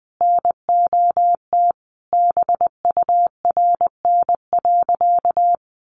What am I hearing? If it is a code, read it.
Morse code, heard as a keyed tone, about 20 wpm, 700 Hz. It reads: NOT BURN.
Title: Sentences from Top 1300 Words 20wpm